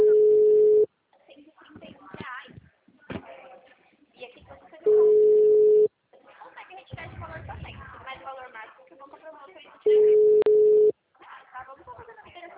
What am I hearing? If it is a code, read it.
Ringback tone (Europe, 425 Hz), with speech.